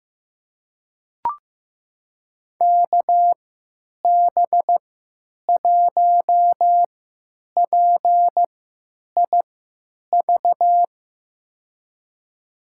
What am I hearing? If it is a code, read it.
Morse code, heard as a keyed tone, about 15 wpm, spaced as for 9 wpm, 700 Hz. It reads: KB1PIV
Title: USA Call Signs - 3x Character Spacing - 15wpm